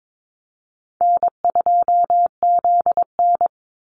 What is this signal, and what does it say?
Morse code, heard as a keyed tone, about 22 wpm, 700 Hz. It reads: N2ZN
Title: Easy Call Signs Repeated 3x - 22wpm